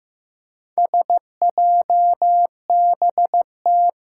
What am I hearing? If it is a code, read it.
Morse code, heard as a keyed tone, about 15 wpm, 700 Hz. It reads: SJBT
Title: Four Challenging Letter-Number Combinations 15wpm